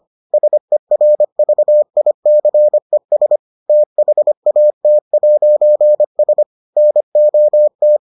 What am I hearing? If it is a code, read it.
Morse code, heard as a keyed tone, about 25 wpm, 600 Hz. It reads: SERVICES THAT'S NOT